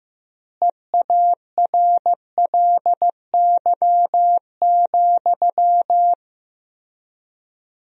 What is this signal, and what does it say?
Morse code, heard as a keyed tone, about 15 wpm, 700 Hz. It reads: EARLY,
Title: 3x Word-Spacing - Sentences from Top 400 Words - 15wpm